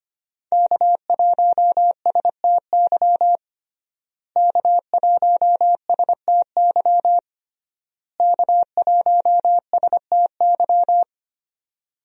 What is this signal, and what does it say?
Morse code, heard as a keyed tone, about 25 wpm, 700 Hz. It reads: K1STY K1STY K1STY
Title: USA Call Signs - 5 Characters - Repeated 3x - 25wpm